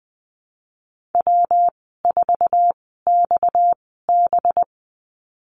Morse code, heard as a keyed tone, about 20 wpm, spaced as for 14 wpm, 700 Hz. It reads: W4XB
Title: 231 Lesson 37g - Letter x - Call Signs 20wpm